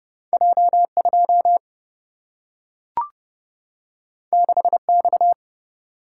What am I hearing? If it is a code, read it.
Morse code, heard as a keyed tone, about 30 wpm, 700 Hz. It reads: J 2 6 X
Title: Single-Letter Single-Number 30wpm